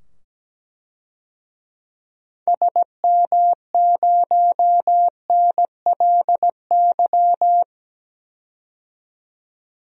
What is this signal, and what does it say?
Morse code, heard as a keyed tone, about 17 wpm, 700 Hz. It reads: SM0NLY
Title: DXCC-100 Call Signs - 6 Characters - 17wpm